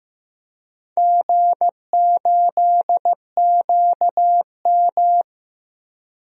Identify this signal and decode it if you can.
Morse code, heard as a keyed tone, about 15 wpm, 700 Hz. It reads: G8QM
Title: DXCC-100 Call Signs - 4 Characters - 15wpm